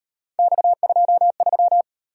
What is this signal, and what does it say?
Morse code, heard as a keyed tone, about 37 wpm, 700 Hz. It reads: X23